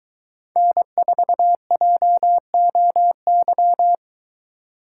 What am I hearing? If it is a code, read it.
Morse code, heard as a keyed tone, about 23 wpm, 700 Hz. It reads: N4JOY